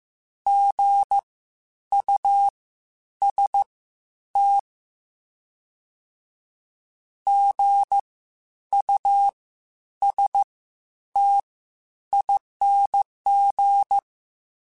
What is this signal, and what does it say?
Morse code, heard as a keyed tone, about 15 wpm, spaced as for 9 wpm, 800 Hz. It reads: GUST GUSTING
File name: Spaced Alphas / Spaced Alphas mp3s / Gs sp @13